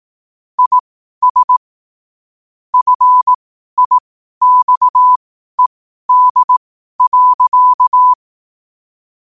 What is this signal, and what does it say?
Morse code, heard as a keyed tone, about 18 wpm, spaced as for 12 wpm, 1000 Hz. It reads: IS FIXED.
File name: qotdfiles / 12wpm